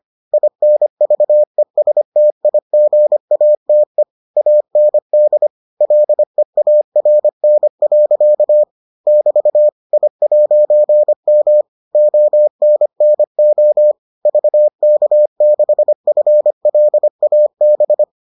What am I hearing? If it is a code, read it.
Morse code, heard as a keyed tone, about 25 wpm, 600 Hz. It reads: INVESTIGATE AND LEARN. <BT> I'M ONNO VK6FLAB